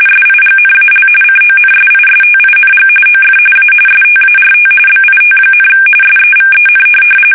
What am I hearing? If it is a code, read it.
RTTY, about 45.45 baud. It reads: SIGNAL CONSISTING OF A START BIT, SEVEN
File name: fsk850 / teletype-corp